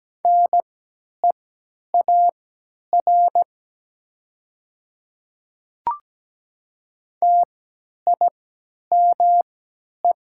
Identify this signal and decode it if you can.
Morse code, heard as a keyed tone, about 17 wpm, spaced as for 10 wpm, 700 Hz. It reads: NEAR TIME
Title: 3x Character Spacing - Top 200 Words - 17wpm